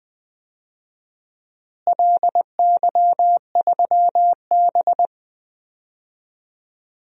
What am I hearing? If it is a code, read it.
Morse code, heard as a keyed tone, about 20 wpm, 700 Hz. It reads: LY3B
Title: DXCC-100 Call Signs - 4 Characters - Repeated 3x - 20wpm